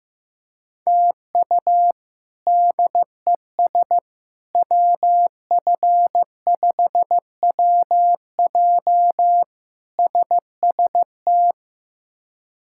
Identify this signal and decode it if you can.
Morse code, heard as a keyed tone, about 15 wpm, 700 Hz. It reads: TU DES WF5WJ SST
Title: K1USN SST QSO Practice - 15wpm